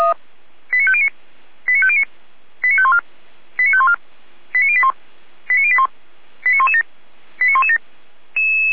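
Five-tone selective call (ZVEI), heard as quick runs of five tones, tones of about 70 ms. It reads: 89509 89509 89514 89514 89081 89081 89117 89117
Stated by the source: ZVEI five-tone selective call received on an SDR.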